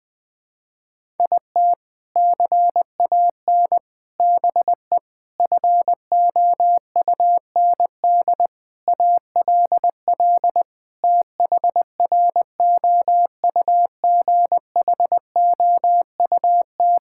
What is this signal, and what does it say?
Morse code, heard as a keyed tone, about 21 wpm, 700 Hz. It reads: IT CAN BE FOUND ALL THROUGHOUT
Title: Sentences from Top 1000 Words - Review Part 5 - 20wpm